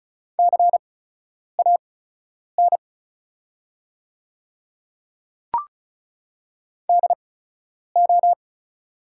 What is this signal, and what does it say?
Morse code, heard as a keyed tone, about 36 wpm, spaced as for 10 wpm, 700 Hz. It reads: CAN DO
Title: 8x Character Spacing - Top 100 Words - 35wpm